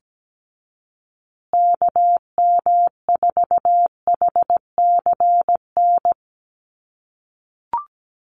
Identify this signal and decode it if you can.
Morse code, heard as a keyed tone, about 17 wpm, 700 Hz. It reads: KM4HCN
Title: USA Call Signs Repeated 2x 17wpm